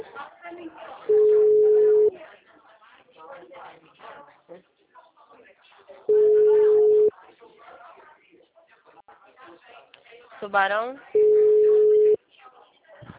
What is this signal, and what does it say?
Ringback tone (Europe, 425 Hz), with speech.